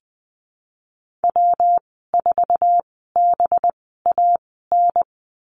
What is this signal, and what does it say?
Morse code, heard as a keyed tone, about 20 wpm, spaced as for 14 wpm, 700 Hz. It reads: W4BAN
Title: 193 Lesson 31g - Letter v - Call Signs 20wpm